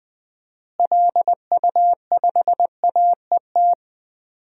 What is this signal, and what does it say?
Morse code, heard as a keyed tone, about 20 wpm, 700 Hz. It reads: LU5AET
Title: DXCC-100 Call Signs - 6 Characters - Repeated 2x - 20wpm